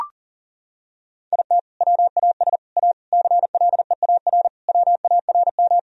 Morse code, heard as a keyed tone, about 37 wpm, 700 Hz. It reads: IT WAS A CLEAR WARM
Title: Sentences from Top 1000 Words - Review Part 2 - 40wpm